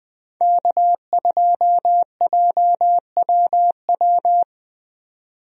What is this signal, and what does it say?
Morse code, heard as a keyed tone, about 20 wpm, 700 Hz. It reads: K2JWW
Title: USA Call Signs - 5 Characters - 20wpm